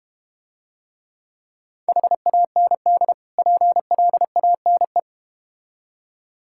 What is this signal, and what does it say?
Morse code, heard as a keyed tone, about 31 wpm, 700 Hz. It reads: HAND PLANE
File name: practice / 25wpm / Difficult Words - Speed-Racing with Hints - 25wpm